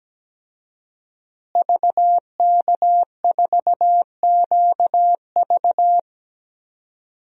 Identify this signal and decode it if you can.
Morse code, heard as a keyed tone, about 17 wpm, 700 Hz. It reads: VK4QV
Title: DXCC-100 Call Signs - 5 Characters - Repeated 3x - 17wpm